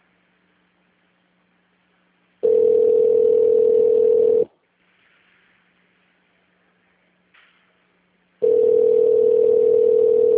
Ringback tone (North America), 440 + 480 Hz.